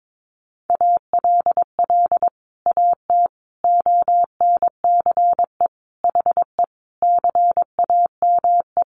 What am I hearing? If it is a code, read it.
Morse code, heard as a keyed tone, about 22 wpm, 700 Hz. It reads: ALL AT ONCE HE CAME
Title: Sentences from Top 1100 Words 22wpm